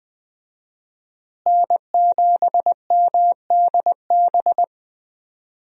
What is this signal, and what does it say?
Morse code, heard as a keyed tone, about 20 wpm, 700 Hz. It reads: N7MDB
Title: USA Call Signs - 5 Characters - Repeated 3x - 20wpm